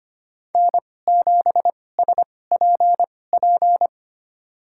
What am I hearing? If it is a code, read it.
Morse code, heard as a keyed tone, about 25 wpm, spaced as for 18 wpm, 700 Hz. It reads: N7SPP
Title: USA Call Signs - 5 Characters - 2x Character Spacing - 25wpm